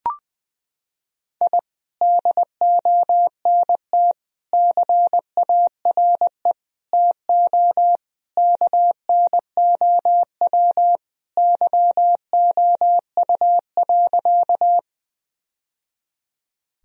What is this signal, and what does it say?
Morse code, heard as a keyed tone, about 20 wpm, 700 Hz. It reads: I DONT CARE TO KNOW YOU.
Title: Sentences from Top 1000 Words - Review Part 1 - 20wpm